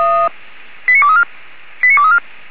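Five-tone selective call (ZVEI), heard as quick runs of five tones, tones of about 70 ms. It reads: 89234 89234
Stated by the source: ZVEI five-tone selective call, code 89241, received via SDR.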